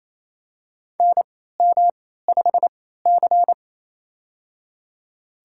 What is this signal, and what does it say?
Morse code, heard as a keyed tone, about 27 wpm, spaced as for 16 wpm, 700 Hz. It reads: NM5C
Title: USA Call Signs - 4 Characters - 3x Character Spacing - 28wpm